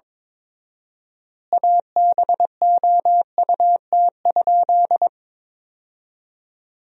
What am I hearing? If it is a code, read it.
Morse code, heard as a keyed tone, about 23 wpm, 700 Hz. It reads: ABOUT?
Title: 4x Word-Spacing - Sentences from Top 100 Words - 22wpm